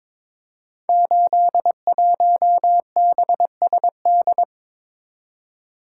Morse code, heard as a keyed tone, about 22 wpm, 700 Hz. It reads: 81BSD
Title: Code Groups - 5 Letter:Numbers - 22wpm